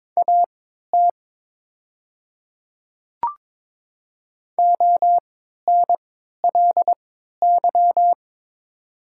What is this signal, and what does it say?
Morse code, heard as a keyed tone, about 22 wpm, spaced as for 13 wpm, 700 Hz. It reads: AT ONLY
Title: 3x Character Spacing - Top 200 Words - 22wpm